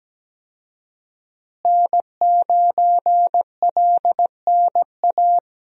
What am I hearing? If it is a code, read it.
Morse code, heard as a keyed tone, about 17 wpm, 700 Hz. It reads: N9LNA
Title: USA Call Signs - 5 Characters - Repeated 2x - 17wpm